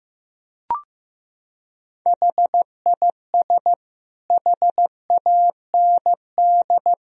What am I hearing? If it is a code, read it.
Morse code, heard as a keyed tone, about 15 wpm, 700 Hz. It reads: HIS HAND
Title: Sentences from Top 1000 Words - Review Part 1 - 15wpm